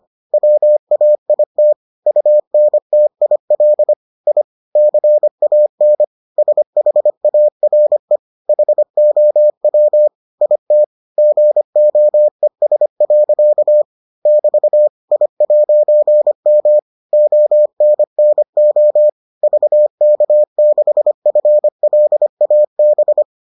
Morse code, heard as a keyed tone, about 25 wpm, 600 Hz. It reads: WAIT UNTIL I CAN SHARE HOW IT GOES. <BT> I'M ONNO VK6FLAB